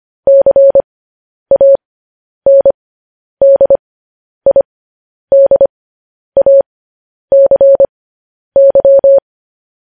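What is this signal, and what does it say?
Morse code, heard as a keyed tone, about 25 wpm, spaced as for 10 wpm, 550 Hz. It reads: CANDIDACY
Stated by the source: Morse code news headlines broadcast at 10 wpm.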